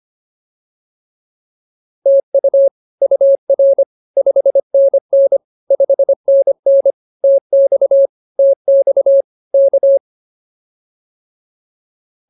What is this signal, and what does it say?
Morse code, heard as a keyed tone, about 24 wpm, 550 Hz. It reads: TU UR 5NN 5NN TX TX K
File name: POTA108a 25